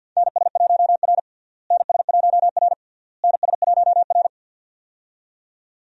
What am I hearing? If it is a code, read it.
Morse code, heard as a keyed tone, about 52 wpm, 700 Hz. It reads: NS1R NS1R NS1R
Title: USA Call Signs - 4 Characters - Repeated 3x - 50wpm